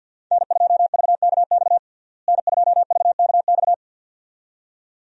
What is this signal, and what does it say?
Morse code, heard as a keyed tone, about 52 wpm, 700 Hz. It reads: N2VKX N2VKX
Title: USA Call Signs - 5 Characters - Repeated 2x - 50wpm